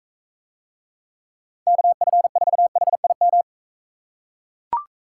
Morse code, heard as a keyed tone, about 40 wpm, 700 Hz. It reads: KF4HIM